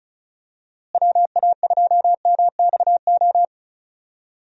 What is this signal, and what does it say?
Morse code, heard as a keyed tone, about 36 wpm, 700 Hz. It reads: WA2MXO
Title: USA Call Signs - 6 Characters - Repeated 2x - 35wpm